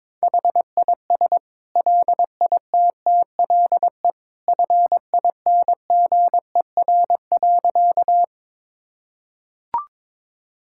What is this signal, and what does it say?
Morse code, heard as a keyed tone, about 22 wpm, 700 Hz. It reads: HIS LITTLE FINGER.
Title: Sentences from Top 1600 Words 22wpm